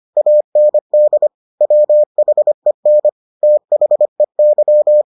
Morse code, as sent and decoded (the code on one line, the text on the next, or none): .- -. -.. / .-- .... -. / - .... -.--
AND WHEN THEY